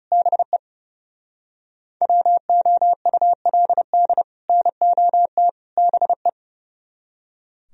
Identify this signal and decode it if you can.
Morse code, heard as a keyed tone, about 39 wpm, 700 Hz. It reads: BE WOULD NOT BE